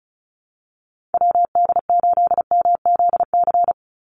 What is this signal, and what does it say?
Morse code, heard as a keyed tone, about 34 wpm, 700 Hz. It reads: WD8MZC
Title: USA Call Signs 35wpm